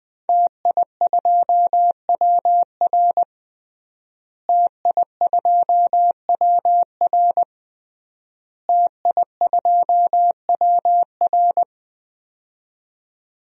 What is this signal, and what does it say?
Morse code, heard as a keyed tone, about 20 wpm, 700 Hz. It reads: TI2WR TI2WR TI2WR
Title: DXCC-100 Call Signs - 5 Characters - Repeated 3x - 20wpm